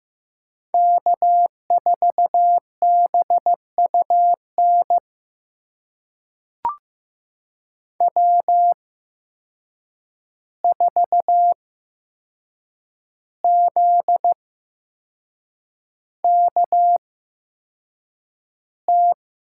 Morse code, heard as a keyed tone, about 15 wpm, spaced as for 6 wpm, 700 Hz. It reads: K4BUN W4ZKT
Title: USA Call Signs - 5 Characters - 8x Character Spacing - 15wpm